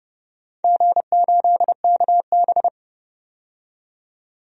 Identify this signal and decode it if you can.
Morse code, heard as a keyed tone, about 29 wpm, 700 Hz. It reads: G8KB
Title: DXCC-100 Call Signs - 4 Characters - 30wpm